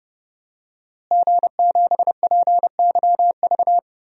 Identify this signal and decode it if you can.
Morse code, heard as a keyed tone, about 30 wpm, 700 Hz. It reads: G7PYV
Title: DXCC-100 Call Signs - 5 Characters - Repeated 2x - 30wpm